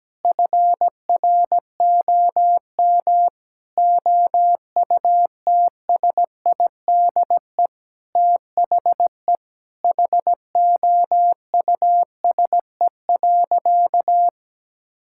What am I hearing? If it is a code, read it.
Morse code, heard as a keyed tone, about 17 wpm, 700 Hz. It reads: FROM OUTSIDE THE HOUSE.
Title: Sentences from Top 1000 Words - Review Part 6 - 17wpm